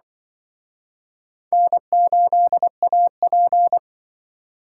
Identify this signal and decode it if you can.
Morse code, heard as a keyed tone, about 26 wpm, 700 Hz. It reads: N8AP